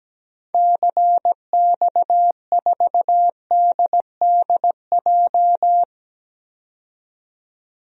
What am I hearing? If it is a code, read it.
Morse code, heard as a keyed tone, about 17 wpm, 700 Hz. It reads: CX4DDJ
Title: DXCC-100 Call Signs - 6 Characters - 17wpm